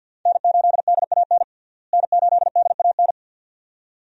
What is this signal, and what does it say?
Morse code, heard as a keyed tone, about 52 wpm, 700 Hz. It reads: N8DAN N8DAN
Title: USA Call Signs - 5 Characters - Repeated 2x - 50wpm